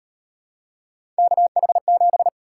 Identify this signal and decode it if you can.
Morse code, heard as a keyed tone, about 33 wpm, 700 Hz. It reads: KH7